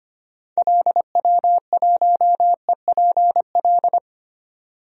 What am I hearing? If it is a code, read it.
Morse code, heard as a keyed tone, about 25 wpm, 700 Hz. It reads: LW1EPL
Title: DXCC-100 Call Signs - 6 Characters - 25wpm